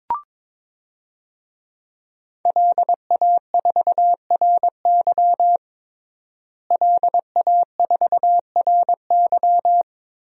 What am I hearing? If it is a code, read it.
Morse code, heard as a keyed tone, about 22 wpm, 700 Hz. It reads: LA4RY LA4RY
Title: DXCC-100 Call Signs - 5 Characters - Repeated 3x - 22wpm